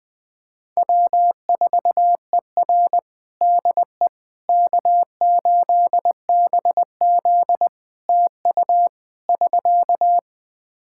Morse code, heard as a keyed tone, about 20 wpm, 700 Hz. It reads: W4ER DE K8BZ TU <SK>
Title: Practice QSO - Exchange 4 - Ending - 20wpm